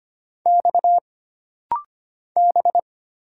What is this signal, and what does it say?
Morse code, heard as a keyed tone, about 25 wpm, 700 Hz. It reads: X B
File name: Single Character (Letter:Number:Punctuation:Prosign) - Rapid-Fire - 25wpm